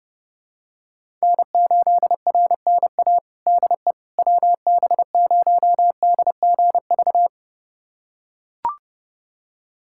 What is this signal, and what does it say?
Morse code, heard as a keyed tone, about 30 wpm, 700 Hz. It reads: N8RNA DE WB0DGV
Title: USA Callsign DE Callsign 30wpm